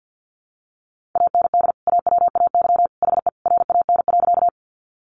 Morse code, heard as a keyed tone, about 50 wpm, 700 Hz. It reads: AND AWAY HE RAN.
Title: Sentences from Top 600 Words 50wpm